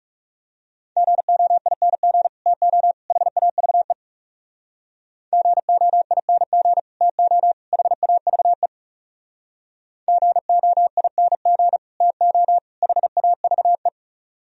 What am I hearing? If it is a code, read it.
Morse code, heard as a keyed tone, about 36 wpm, 700 Hz. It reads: GOING TO HAVE GOING TO HAVE GOING TO HAVE
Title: Speed-Racing - Sets of 3 Words 30wpm